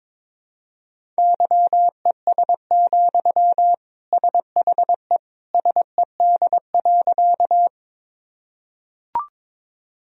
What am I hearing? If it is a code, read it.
Morse code, heard as a keyed tone, about 22 wpm, 700 Hz. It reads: YES, SHE SED.
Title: Sentences from Top 500 Words and Abbreviations 22wpm